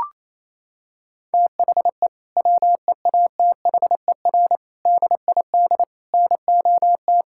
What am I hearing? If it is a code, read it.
Morse code, heard as a keyed tone, about 27 wpm, 700 Hz. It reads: THE WEATHER DID NOT
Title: Sentences from Top 1000 Words - Review Part 5 - 28wpm